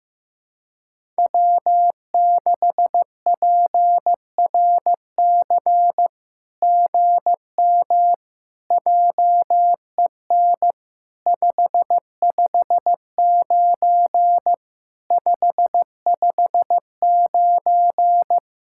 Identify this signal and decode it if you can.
Morse code, heard as a keyed tone, about 15 wpm, 700 Hz. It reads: W6PRC GM JEN 559 559
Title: SOTA QSO Practice 15wpm